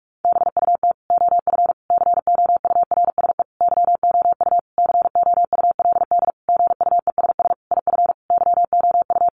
Morse code, heard as a keyed tone, about 44 wpm, 700 Hz. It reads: BUT OF COURSE YOU COULD GUESS IF YOU
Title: Sentences from Top 1500 Words 45wpm